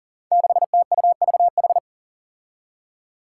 Morse code, heard as a keyed tone, about 39 wpm, 700 Hz. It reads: 6TUVH
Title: Five Challenging Letter-Number Combinations 40wpm